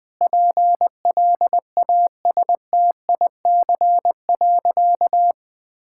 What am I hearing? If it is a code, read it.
Morse code, heard as a keyed tone, about 20 wpm, 700 Hz. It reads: PLASTIC.
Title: Easy Sentences with Hints - 20wpm